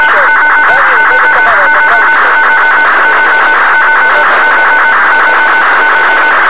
Two-tone teleprinter signal, about 200 Hz shift.